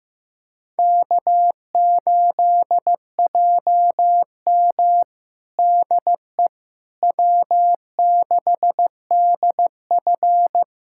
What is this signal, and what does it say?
Morse code, heard as a keyed tone, about 15 wpm, 700 Hz. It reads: K8JM DE W6DF
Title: USA Callsign DE Callsign - 4 Characters - 15wpm